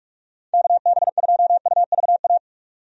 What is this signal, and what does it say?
Morse code, heard as a keyed tone, about 46 wpm, 700 Hz. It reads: KB2UVA
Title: USA Call Signs - 6 Characters - 45wpm